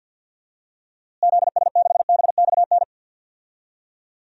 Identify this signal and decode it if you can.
Morse code, heard as a keyed tone, about 50 wpm, 700 Hz. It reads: ZS6BXN
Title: DXCC-100 Call Signs - 6 Characters - Repeated 3x - 50wpm